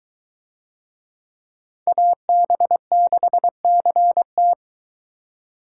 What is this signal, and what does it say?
Morse code, heard as a keyed tone, about 22 wpm, 700 Hz. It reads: AB6CT